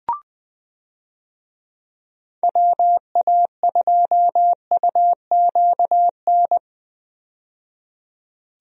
Morse code, heard as a keyed tone, about 20 wpm, 700 Hz. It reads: WA2UQN
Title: USA Call Signs - 6 Characters - 20wpm